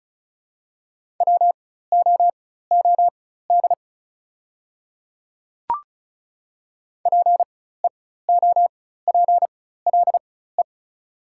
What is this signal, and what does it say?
Morse code, heard as a keyed tone, about 36 wpm, spaced as for 16 wpm, 690 Hz. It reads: WOOD PEOPLE
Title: 4x Character Spacing - Top 300 Words - 35wpm